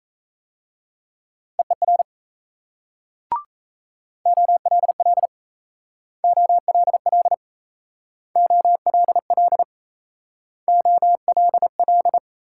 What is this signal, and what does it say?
Morse code, heard as a keyed tone, about 37 wpm, 700 Hz. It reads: EER OLL OLL OLL OLL